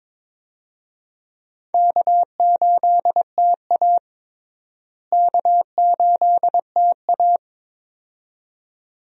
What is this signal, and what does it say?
Morse code, heard as a keyed tone, about 22 wpm, 700 Hz. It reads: K8TA K8TA
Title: USA Call Signs - 4 Characters - Repeated 2x - 22wpm